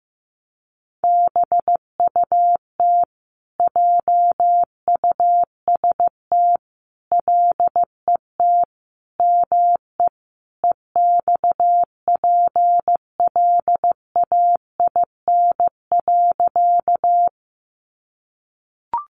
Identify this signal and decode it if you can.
Morse code, heard as a keyed tone, about 14 wpm, 700 Hz. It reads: BUT JUST LET ME EXPLAIN.
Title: Sentences from Top 1100 Words 15wpm